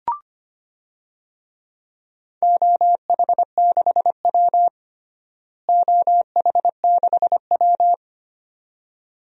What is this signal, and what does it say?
Morse code, heard as a keyed tone, about 25 wpm, 700 Hz. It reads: OH6W OH6W
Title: DXCC-100 Call Signs - 4 Characters - Repeated 2x - 25wpm